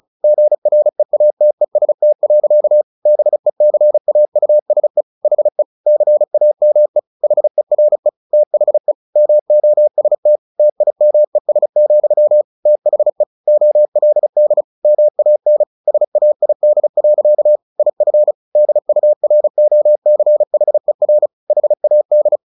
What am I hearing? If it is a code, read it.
Morse code, heard as a keyed tone, about 34 wpm, 600 Hz. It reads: GREATEST. BECAUSE HE CAME HERE THE MOST TIMES, THE OLD MAN SAID. IF DUROCHER HAD
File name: the old man and the sea / the old man and the sea 1 35wpm